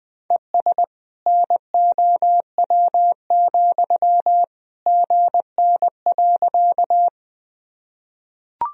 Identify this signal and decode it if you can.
Morse code, heard as a keyed tone, about 20 wpm, 700 Hz. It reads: ES NOW, GN.
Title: Sentences from Top 300 Words and Abbreviations 20wpm